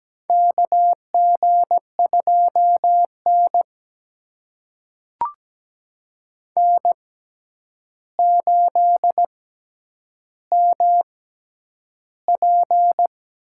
Morse code, heard as a keyed tone, about 17 wpm, spaced as for 9 wpm, 700 Hz. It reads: KG2N N8MP
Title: USA Call Signs - 4 Characters - 6x Character Spacing - 17wpm